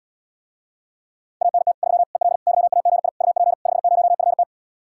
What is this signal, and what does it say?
Morse code, heard as a keyed tone, about 77 wpm, 700 Hz. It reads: ANT 3 EL YAGI UP 50FT